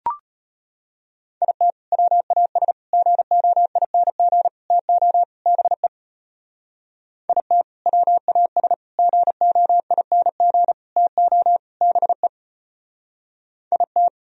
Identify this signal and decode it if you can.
Morse code, heard as a keyed tone, about 29 wpm, 700 Hz. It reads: IT WAS GOING TO BE IT WAS GOING TO BE IT
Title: Speed-Racing - Sets of 5 Words 25wpm